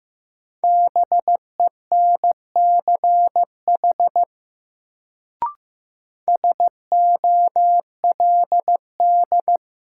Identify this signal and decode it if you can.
Morse code, heard as a keyed tone, about 15 wpm, 700 Hz. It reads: BENCH SOLD
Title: Sentences from Top 1400 Words - New Word Drill Alt - 15wpm